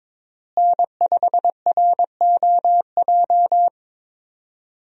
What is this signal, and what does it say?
Morse code, heard as a keyed tone, about 22 wpm, 700 Hz. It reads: N5ROJ
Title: USA Call Signs - 5 Characters - Repeated 2x - 22wpm